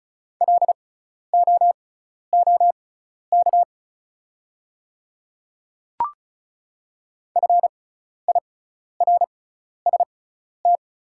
Morse code, heard as a keyed tone, about 34 wpm, spaced as for 12 wpm, 700 Hz. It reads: LOOK FIRST
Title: 6x Character Spacing - Top 100 Words - 35wpm